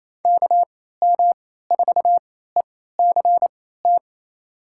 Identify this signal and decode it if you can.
Morse code, heard as a keyed tone, about 27 wpm, spaced as for 16 wpm, 700 Hz. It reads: KM4ECT
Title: USA Call Signs - 3x Character Spacing - 28wpm